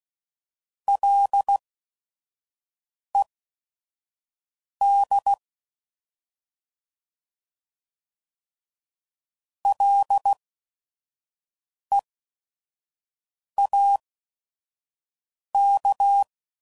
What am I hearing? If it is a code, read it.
Morse code, heard as a keyed tone, about 16 wpm, spaced as for 5 wpm, 800 Hz. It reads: LED LEAK
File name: Prefixes / By First Letter / Ls